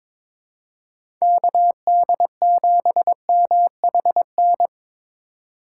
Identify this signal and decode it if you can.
Morse code, heard as a keyed tone, about 22 wpm, 700 Hz. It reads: KD7MHN
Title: USA Call Signs - 3x Character Spacing - 22wpm